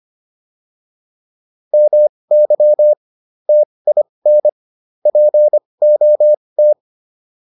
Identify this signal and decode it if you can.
Morse code, heard as a keyed tone, about 25 wpm, spaced as for 20 wpm, 600 Hz. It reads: MY TIN POT